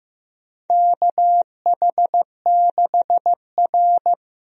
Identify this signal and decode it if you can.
Morse code, heard as a keyed tone, about 15 wpm, 700 Hz. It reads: KH6R
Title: USA Call Signs - 4 Characters - Repeated 3x - 15wpm